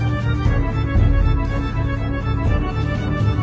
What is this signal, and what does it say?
There is no code.